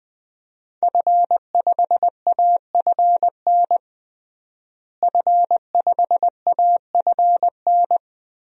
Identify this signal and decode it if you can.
Morse code, heard as a keyed tone, about 20 wpm, 700 Hz. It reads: F5AFN F5AFN
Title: DXCC-100 Call Signs - 5 Characters - Repeated 3x - 20wpm